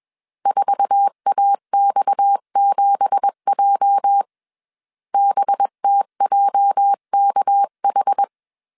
Morse code, heard as a keyed tone, about 21 wpm, 800 Hz. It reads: 4AX7J BTJKH
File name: CW / DET / 1 9 1500C S B 20wpm 2313